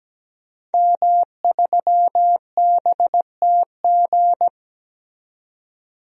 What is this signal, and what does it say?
Morse code, heard as a keyed tone, about 17 wpm, 700 Hz. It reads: M3BTG
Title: DXCC-100 Call Signs - 5 Characters - 17wpm